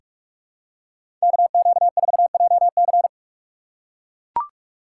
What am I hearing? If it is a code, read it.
Morse code, heard as a keyed tone, about 42 wpm, 700 Hz. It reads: KQ4JC
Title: USA Call Signs - 5 Characters - 3x Character Spacing - 45wpm